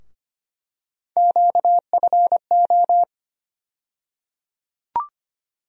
Morse code, heard as a keyed tone, about 25 wpm, 700 Hz. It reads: QFO
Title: USA Call Sign Postfixes - 3 Letters 25wpm